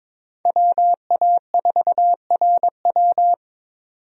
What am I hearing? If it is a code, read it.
Morse code, heard as a keyed tone, about 22 wpm, 700 Hz. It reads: WA4RW
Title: USA Call Signs - 5 Characters - 22wpm